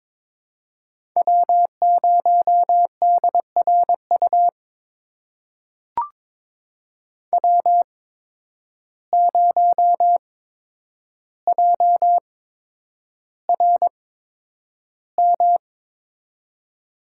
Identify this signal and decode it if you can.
Morse code, heard as a keyed tone, about 21 wpm, spaced as for 9 wpm, 700 Hz. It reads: W0DRU W0JRM
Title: USA Call Signs - 5 Characters - 8x Character Spacing - 22wpm